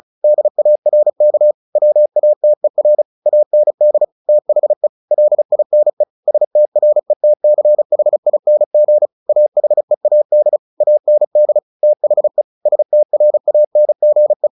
Morse code, heard as a keyed tone, about 34 wpm, 600 Hz. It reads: DARK WATER AND THE LINE STRETCHING AHEAD AND THE STRANGE
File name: the old man and the sea / the old man and the sea 1 35wpm